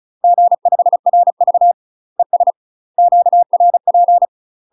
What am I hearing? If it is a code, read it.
Morse code, heard as a keyed tone, about 36 wpm, 700 Hz. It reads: G5RV ES QRP
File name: UK MASTER QSOs TRAINING / 35WPM / UK MASTER QSOs 0035